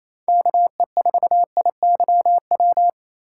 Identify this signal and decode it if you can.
Morse code, heard as a keyed tone, about 28 wpm, 700 Hz. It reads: KE4IYW
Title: USA Call Signs - 2x Character Spacing - 28wpm